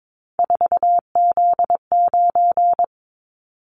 Morse code, heard as a keyed tone, about 22 wpm, 700 Hz. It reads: 4Z9
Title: Callsign Prefixes for Top 50 DXCC - 22wpm